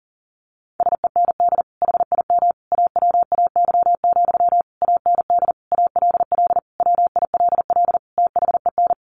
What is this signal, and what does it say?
Morse code, heard as a keyed tone, about 40 wpm, 700 Hz. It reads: SEND HIM AWAY, AND ALL WILL THEN
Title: Sentences from Top 1500 Words 40wpm